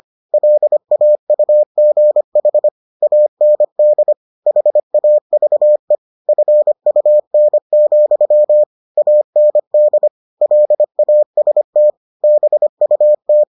Morse code, heard as a keyed tone, about 25 wpm, 600 Hz. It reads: LAUGH AND HAVE FUN, AND LAST BUT